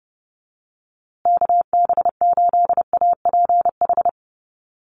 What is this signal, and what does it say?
Morse code, heard as a keyed tone, about 30 wpm, 700 Hz. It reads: KB8APH
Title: USA Call Signs Repeated 3x 30wpm